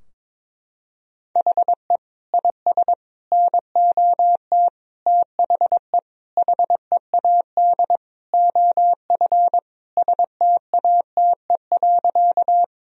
Morse code, heard as a keyed tone, about 22 wpm, 700 Hz. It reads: HE IS NOT THE HEAD OF STATE.
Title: Sentences from Top 1000 Words - Review Part 2 - 22wpm